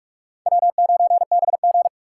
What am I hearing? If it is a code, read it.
Morse code, heard as a keyed tone, about 43 wpm, 700 Hz. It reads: W9BG
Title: USA Call Signs - 4 Characters - 2x Character Spacing - 45wpm